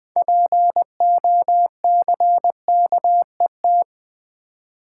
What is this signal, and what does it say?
Morse code, heard as a keyed tone, about 20 wpm, 690 Hz. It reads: POCKET
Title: 6 Letter Words - First 3 Letters Given - 20wpm